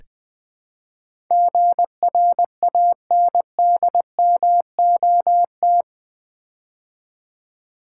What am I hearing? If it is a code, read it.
Morse code, heard as a keyed tone, about 20 wpm, 700 Hz. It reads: GRANDMOT
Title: Build A Word - 4 Letters At A Time - 11-Letter Words - 20wpm